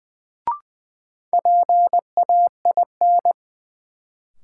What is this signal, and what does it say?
Morse code, heard as a keyed tone, about 20 wpm, 700 Hz. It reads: PAIN
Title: Sentences from Top 1600 Words - New Word Drill Alt - 20wpm